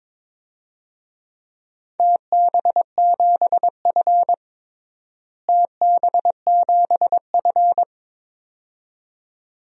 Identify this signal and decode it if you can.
Morse code, heard as a keyed tone, about 22 wpm, 700 Hz. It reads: TB7F TB7F
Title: DXCC-100 Call Signs - 4 Characters - Repeated 2x - 22wpm